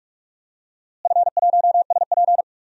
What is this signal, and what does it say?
Morse code, heard as a keyed tone, about 44 wpm, 700 Hz. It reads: F1SP
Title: DXCC-100 Call Signs - 4 Characters - Repeated 2x - 45wpm